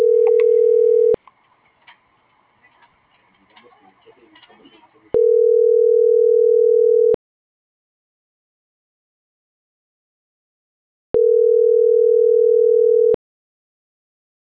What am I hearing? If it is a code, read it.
Ringback tone (North America), 440 + 480 Hz.